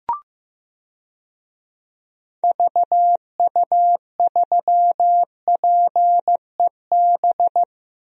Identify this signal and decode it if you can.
Morse code, heard as a keyed tone, about 14 wpm, 700 Hz. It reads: VU3PEB
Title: DXCC-100 Call Signs - 6 Characters - Repeated 3x - 15wpm